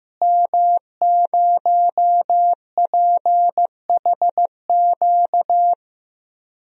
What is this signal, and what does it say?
Morse code, heard as a keyed tone, about 15 wpm, 700 Hz. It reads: M0PHQ
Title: DXCC-100 Call Signs - 5 Characters - Repeated 3x - 15wpm